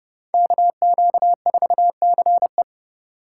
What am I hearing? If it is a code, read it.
Morse code, heard as a keyed tone, about 38 wpm, 700 Hz. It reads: KQ4CE